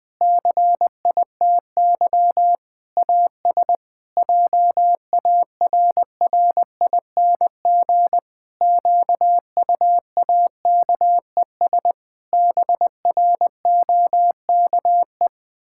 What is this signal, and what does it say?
Morse code, heard as a keyed tone, about 20 wpm, 700 Hz. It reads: CITY AS JARRING QUAKES BROKE
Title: Pangram Challenge v2 - 20wpm